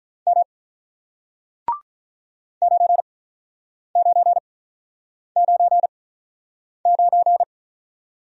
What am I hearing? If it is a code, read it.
Morse code, heard as a keyed tone, about 47 wpm, spaced as for 9 wpm, 700 Hz. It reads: M 9999